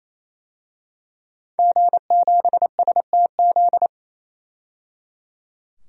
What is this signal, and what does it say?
Morse code, heard as a keyed tone, about 28 wpm, 700 Hz. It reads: G7STZ
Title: DXCC-100 Call Signs - 5 Characters - 28wpm